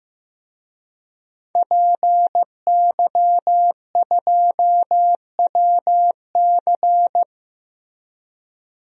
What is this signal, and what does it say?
Morse code, heard as a keyed tone, about 15 wpm, 700 Hz. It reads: PY2WC
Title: DXCC-100 Call Signs - 5 Characters - Repeated 2x - 15wpm